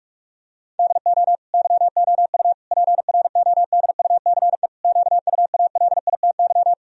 Morse code, heard as a keyed tone, about 43 wpm, 700 Hz. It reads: DO YOU PRODUCE QUALITY
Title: Sentences from Top 1000 Words - Review Part 4 - 45wpm